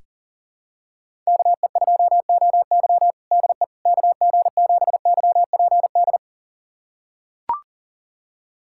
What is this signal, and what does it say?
Morse code, heard as a keyed tone, about 40 wpm, 700 Hz. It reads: KE2OY DE KG7YPD
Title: USA Callsign DE Callsign 40wpm